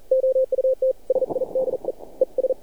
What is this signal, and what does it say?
Morse code, heard as a keyed tone, about 46 wpm, spaced as for 38 wpm, 520 Hz. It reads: OUTSIDE ES
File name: archived / CRSnet